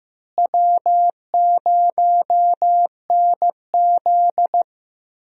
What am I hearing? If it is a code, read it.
Morse code, heard as a keyed tone, about 15 wpm, 700 Hz. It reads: W0NZ
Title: USA Call Signs - 4 Characters - Repeated 3x - 15wpm